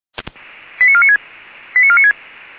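Five-tone selective call (ZVEI), heard as quick runs of five tones, tones of about 70 ms. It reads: 89496 89496
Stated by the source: ZVEI five-tone selective call 89155 received by SDR.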